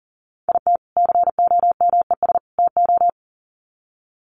Morse code, heard as a keyed tone, about 40 wpm, 700 Hz. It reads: IT COMES TO
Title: Sets of 3 Words 40wpm